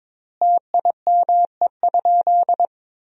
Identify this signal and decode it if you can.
Morse code, heard as a keyed tone, about 22 wpm, 700 Hz. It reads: TIME?
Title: 4x Word-Spacing - Sentences from Top 400 Words - 22wpm